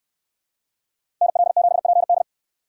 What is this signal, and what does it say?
Morse code, heard as a keyed tone, about 69 wpm, 700 Hz. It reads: NL7PD